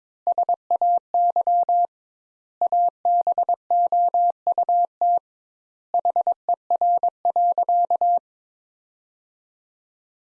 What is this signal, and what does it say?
Morse code, heard as a keyed tone, about 21 wpm, 700 Hz. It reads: SAY ABOUT HER.
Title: Sentences Top 200 Words - 22wpm